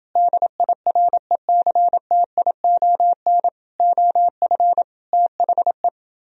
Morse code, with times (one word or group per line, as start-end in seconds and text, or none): 0.2-3.5 DIRECTION
3.8-4.8 OF
5.1-5.9 THE